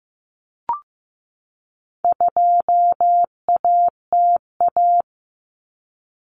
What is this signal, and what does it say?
Morse code, heard as a keyed tone, about 15 wpm, 700 Hz. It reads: 2ATA
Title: Common Cut Numbers - 4-Digit 15wpm